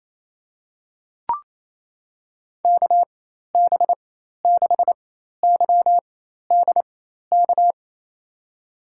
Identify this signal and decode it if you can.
Morse code, heard as a keyed tone, about 28 wpm, spaced as for 13 wpm, 700 Hz. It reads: KB6YDK
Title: USA Call Signs - 6 Characters - 4x Character Spacing - 28wpm